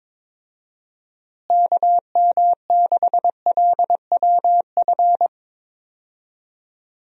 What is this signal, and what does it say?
Morse code, heard as a keyed tone, about 22 wpm, 700 Hz. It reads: KM6LWF
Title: USA Call Signs - 6 Characters - Repeated 2x - 22wpm